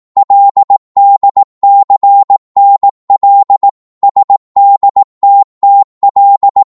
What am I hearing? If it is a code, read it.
Morse code, heard as a keyed tone, about 18 wpm, 800 Hz. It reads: LDCNL SDTTL
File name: Morse / ltr-18wpm